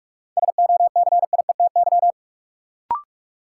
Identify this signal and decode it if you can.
Morse code, heard as a keyed tone, about 46 wpm, 700 Hz. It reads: SOCIETY
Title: Sentences from Top 700 Words - New Word Drill - 45wpm